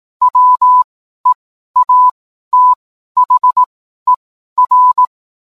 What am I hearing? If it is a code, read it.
Morse code, heard as a keyed tone, about 18 wpm, spaced as for 12 wpm, 1000 Hz. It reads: WEATHER
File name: cwblog / cw mp3 directory / mem morse 12